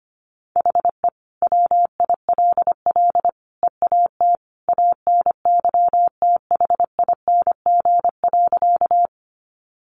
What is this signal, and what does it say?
Morse code, heard as a keyed tone, about 25 wpm, 700 Hz. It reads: HE WILL EAT ANYTHING.
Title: Sentences from Top 1000 Words 25wpm